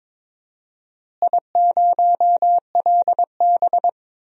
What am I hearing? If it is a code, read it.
Morse code, heard as a keyed tone, about 22 wpm, 700 Hz. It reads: I0LB
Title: DXCC-100 Call Signs - 4 Characters - Repeated 3x - 22wpm